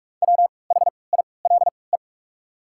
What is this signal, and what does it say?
Morse code, heard as a keyed tone, about 48 wpm, 700 Hz. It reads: W H I L E
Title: 3x Character Spacing - Top 300 Words - 45wpm